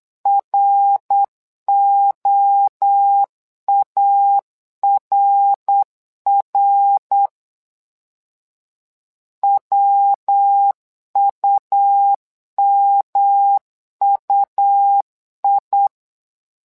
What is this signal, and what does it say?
Morse code, heard as a keyed tone, about 8 wpm, 800 Hz. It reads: ROARR WUMUI
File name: LEZIONI / 3 MWRO / 3 MWRO 8wpm 2606